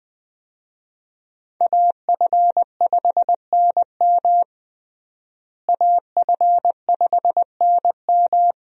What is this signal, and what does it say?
Morse code, heard as a keyed tone, about 20 wpm, 700 Hz. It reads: AF5NM AF5NM
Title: USA Call Signs - 5 Characters - Repeated 2x - 20wpm